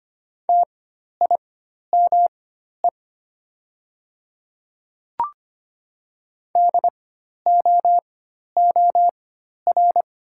Morse code, heard as a keyed tone, about 25 wpm, spaced as for 12 wpm, 700 Hz. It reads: TIME DOOR
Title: 4x Character Spacing - Top 300 Words - 25wpm